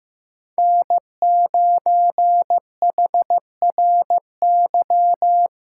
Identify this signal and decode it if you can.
Morse code, heard as a keyed tone, about 15 wpm, 700 Hz. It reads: N9HRY
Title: USA Callsign DE Callsign - 5 Characters - 15wpm